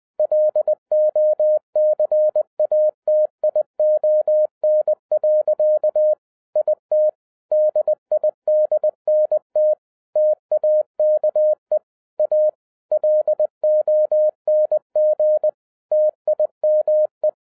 Morse code, heard as a keyed tone, about 20 wpm, 610 Hz. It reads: LOCATION. IT DIDNT TAKE A LONG TIME
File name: 2012-01-mp3 / n05-CAME BACK TO HAM RADIO AFTER 30 YEARS OF INTERVAL-20wpm